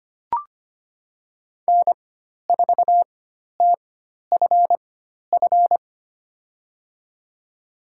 Morse code, heard as a keyed tone, about 25 wpm, spaced as for 12 wpm, 700 Hz. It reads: N4TFF
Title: USA Call Signs - 4x Character Spacing - 25wpm